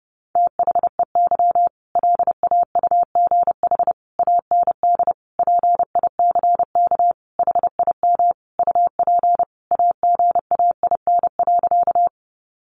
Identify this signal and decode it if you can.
Morse code, heard as a keyed tone, about 30 wpm, 700 Hz. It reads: THEY LAUGH AND PICK HIM UP AGAIN.
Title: Sentences from Top 1000 Words 30wpm